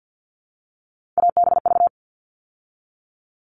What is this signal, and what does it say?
Morse code, heard as a keyed tone, about 50 wpm, 700 Hz. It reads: ABV
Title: All Cut Numbers - 3-Digit 50wpm